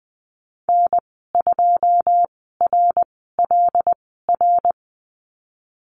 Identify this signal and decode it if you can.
Morse code, heard as a keyed tone, about 20 wpm, spaced as for 15 wpm, 700 Hz. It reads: N2RLR
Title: 128 Lesson 21g - Number 6 - Call Signs 20wpm